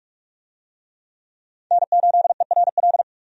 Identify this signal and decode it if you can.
Morse code, heard as a keyed tone, about 46 wpm, 700 Hz. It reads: N8ERL
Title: USA Call Signs - 5 Characters - 45wpm